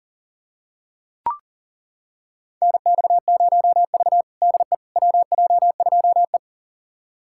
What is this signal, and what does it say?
Morse code, heard as a keyed tone, about 40 wpm, 700 Hz. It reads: NX0V DE WJ2E
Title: USA Callsign DE Callsign - 4 Characters - 40wpm